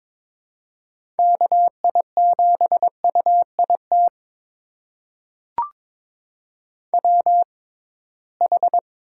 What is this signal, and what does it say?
Morse code, heard as a keyed tone, about 22 wpm, spaced as for 17 wpm, 700 Hz. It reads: KI7UIT WH
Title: USA Call Signs - 6 Characters - 6x Character Spacing - 22wpm